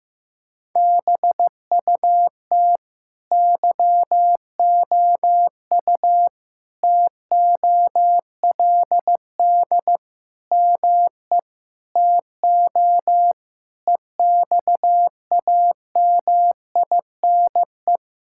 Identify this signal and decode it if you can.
Morse code, heard as a keyed tone, about 15 wpm, 700 Hz. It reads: BUT YOU TOLD ME TO EXAMINE
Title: Sentences from Top 1600 Words 15wpm